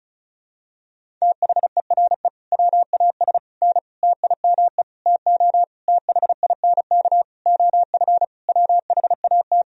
Morse code, heard as a keyed tone, about 34 wpm, 700 Hz. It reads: THERE WAS N TIME TO THINK OF WHAT
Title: Sentences from Top 400 Words and Abbreviations 35wpm